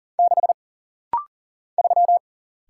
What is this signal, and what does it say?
Morse code, heard as a keyed tone, about 40 wpm, 700 Hz. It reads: 6 3
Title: Single Letter-Number - Rapid-Fire 40wpm